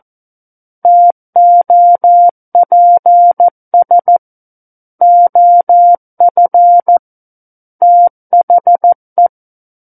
Morse code, heard as a keyed tone, about 14 wpm, 700 Hz. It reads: TOPS OF THE
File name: Book One / 05-Chapter 5 - 14 WPM